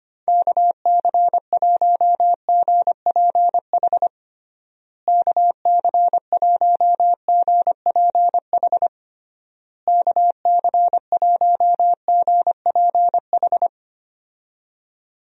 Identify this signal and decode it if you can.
Morse code, heard as a keyed tone, about 25 wpm, 700 Hz. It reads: KC1GPH KC1GPH KC1GPH
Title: USA Call Signs - 6 Characters - Repeated 3x - 25wpm